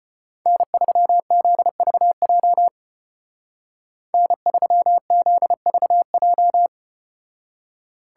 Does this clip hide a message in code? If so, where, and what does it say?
Yes: Morse code from 0.5 to 6.7 s, heard as a keyed tone, about 32 wpm, 700 Hz. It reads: N3ZVJ N3ZVJ